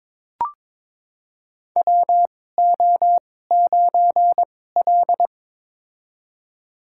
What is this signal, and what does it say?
Morse code, heard as a keyed tone, about 22 wpm, spaced as for 16 wpm, 700 Hz. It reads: WO9L
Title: USA Call Signs - 4 Characters - 2x Character Spacing - 22wpm